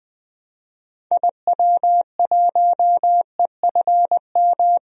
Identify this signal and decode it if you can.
Morse code, heard as a keyed tone, about 20 wpm, 700 Hz. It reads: IW1EFM
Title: DXCC-100 Call Signs - 6 Characters - Repeated 2x - 20wpm